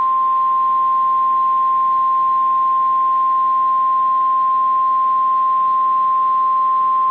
Emergency Alert System (SAME) tones (the 1050 Hz attention tone). It is heard from 0.5 to 5.9 s.
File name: EAS Logs / KMNR